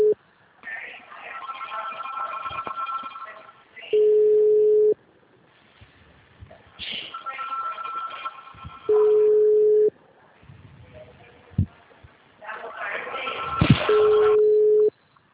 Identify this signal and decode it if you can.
Ringback tone (Europe, 425 Hz).
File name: out- -2025- - - .44481